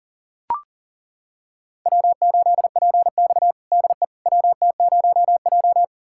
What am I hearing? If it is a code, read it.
Morse code, heard as a keyed tone, about 40 wpm, 700 Hz. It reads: W8PX DE WT0J
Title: USA Callsign DE Callsign - 4 Characters - 40wpm